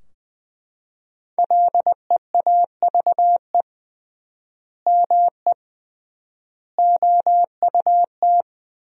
Morse code, heard as a keyed tone, about 21 wpm, 700 Hz. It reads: LEAVE ME OUT
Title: 3x Word-Spacing - Sentences from Top 400 Words - 20wpm